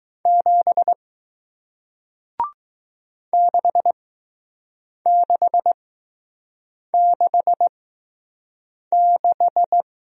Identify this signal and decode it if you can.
Morse code, heard as a keyed tone, about 20 wpm, spaced as for 6 wpm, 700 Hz. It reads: 7 6666